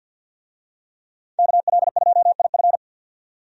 Morse code, heard as a keyed tone, about 52 wpm, 700 Hz. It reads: KL2IF
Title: USA Call Signs - 5 Characters - 50wpm